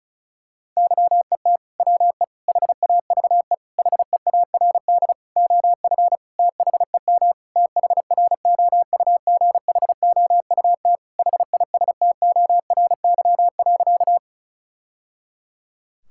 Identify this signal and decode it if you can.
Morse code, heard as a keyed tone, about 34 wpm, 700 Hz. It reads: YET WE HAVE HEARD OF THEM THROUGHOUT HISTORY.
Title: Sentences from Top 1000 Words - Review Part 5 - 35wpm